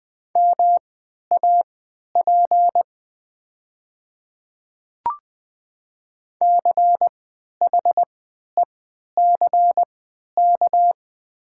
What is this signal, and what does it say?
Morse code, heard as a keyed tone, about 20 wpm, spaced as for 11 wpm, 700 Hz. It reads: MAP CHECK
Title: 3x Character Spacing - Top 500 Words - 20wpm